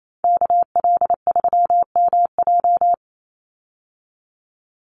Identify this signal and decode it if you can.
Morse code, heard as a keyed tone, about 28 wpm, 700 Hz. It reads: KL3MJ
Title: USA Call Signs 28wpm